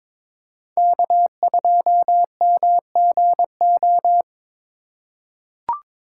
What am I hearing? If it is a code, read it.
Morse code, heard as a keyed tone, about 21 wpm, 700 Hz. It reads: K2MGO
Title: USA Call Signs - 5 Characters - 4x Character Spacing - 22wpm